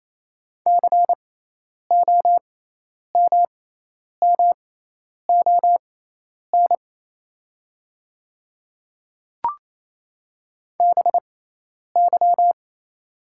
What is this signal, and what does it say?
Morse code, heard as a keyed tone, about 28 wpm, spaced as for 10 wpm, 700 Hz. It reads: COMMON BY
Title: 6x Character Spacing - Top 500 Words - 28wpm